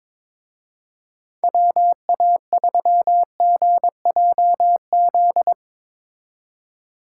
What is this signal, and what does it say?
Morse code, heard as a keyed tone, about 24 wpm, 700 Hz. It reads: WA3GJZ
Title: Speed Racing - USA Call Signs 22wpm